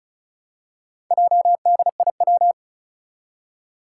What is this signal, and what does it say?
Morse code, heard as a keyed tone, about 36 wpm, 700 Hz. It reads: JDIW
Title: Four Challenging Letter-Number Combinations 35wpm